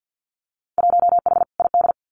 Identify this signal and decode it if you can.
Morse code, heard as a keyed tone, about 50 wpm, 700 Hz. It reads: 1H ID
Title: Winter Field Day Exchanges 50wpm